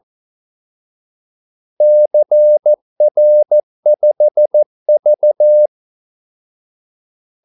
Morse code, heard as a keyed tone, about 14 wpm, 600 Hz. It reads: CR5V
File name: Callsigns-14wpm